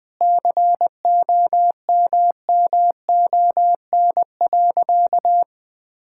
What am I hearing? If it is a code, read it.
Morse code, heard as a keyed tone, about 20 wpm, 700 Hz. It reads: COMMON.
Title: Sentences with 6-Word Phrases - 20wpm